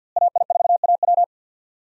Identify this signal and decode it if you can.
Morse code, heard as a keyed tone, about 46 wpm, 700 Hz. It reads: AI4AW
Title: USA Call Signs - 5 Characters - Repeated 3x - 50wpm